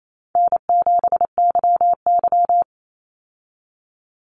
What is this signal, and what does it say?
Morse code, heard as a keyed tone, about 28 wpm, 700 Hz. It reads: N7YY
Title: Easy Call Signs Repeated 2x - 28wpm